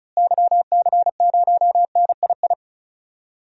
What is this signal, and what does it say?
Morse code, heard as a keyed tone, about 36 wpm, 700 Hz. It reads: YC0NII
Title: DXCC-100 Call Signs - 6 Characters - Repeated 2x - 35wpm